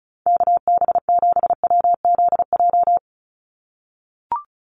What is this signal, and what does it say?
Morse code, heard as a keyed tone, about 34 wpm, 700 Hz. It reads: KB7WZJ
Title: USA Call Signs 35wpm